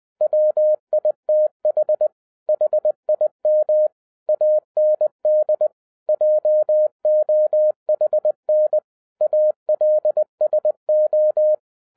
Morse code, heard as a keyed tone, about 20 wpm, 610 Hz. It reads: WITH HIM AND JOHN ALSO